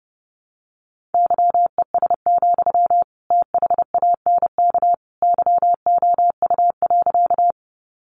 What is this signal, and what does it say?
Morse code, heard as a keyed tone, about 30 wpm, 700 Hz. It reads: YES, THANK YOU.
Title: Sentences from Top 600 Words 30wpm